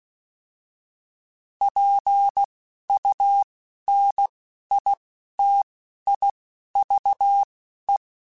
Morse code, heard as a keyed tone, about 16 wpm, 800 Hz. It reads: P U N I T I V E
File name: Alphabetical / Alphabetical mp3s / Ps @10